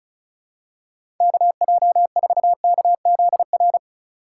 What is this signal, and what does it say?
Morse code, heard as a keyed tone, about 34 wpm, 700 Hz. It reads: KJ4KZR
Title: USA Call Signs - 6 Characters - 8x Character Spacing - 35wpm